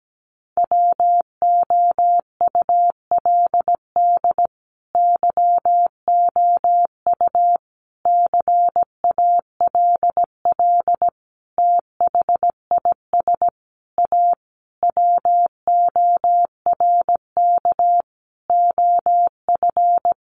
Morse code, heard as a keyed tone, about 17 wpm, 700 Hz. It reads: WOULD YOU CALL THIS A WORK OF
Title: Sentences from Top 600 Words 17wpm